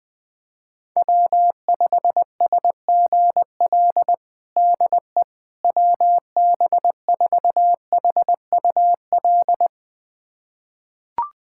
Morse code, heard as a keyed tone, about 21 wpm, 700 Hz. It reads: W5SGL DE WB4HUL
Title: USA Callsign DE Callsign 20wpm